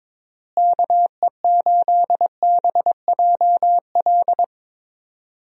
Morse code, heard as a keyed tone, about 22 wpm, 700 Hz. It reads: KE8BJL
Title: USA Call Signs - 6 Characters - Repeated 2x - 22wpm